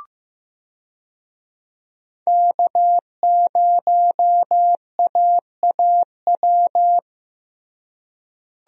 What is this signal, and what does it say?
Morse code, heard as a keyed tone, about 15 wpm, 700 Hz. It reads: K0AAW
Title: USA Call Signs - 5 Characters - Repeated 2x - 15wpm